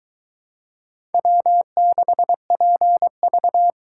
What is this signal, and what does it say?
Morse code, heard as a keyed tone, about 22 wpm, 700 Hz. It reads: W6PV